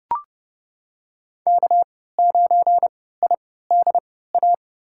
Morse code, heard as a keyed tone, about 32 wpm, spaced as for 17 wpm, 700 Hz. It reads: K9IDA
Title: USA Call Signs - 5 Characters - 3x Character Spacing - 30wpm